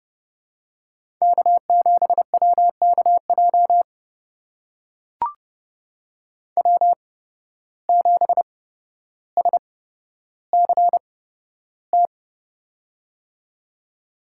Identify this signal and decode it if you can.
Morse code, heard as a keyed tone, about 30 wpm, spaced as for 13 wpm, 700 Hz. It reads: K7WKJ W7SCT
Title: USA Call Signs - 5 Characters - 8x Character Spacing - 30wpm